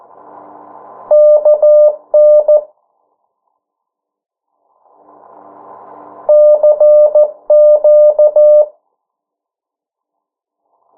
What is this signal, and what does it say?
Morse code, heard as a keyed tone, about 14 wpm, 600 Hz. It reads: KN CQ
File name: morse mp3 / 09 urok 8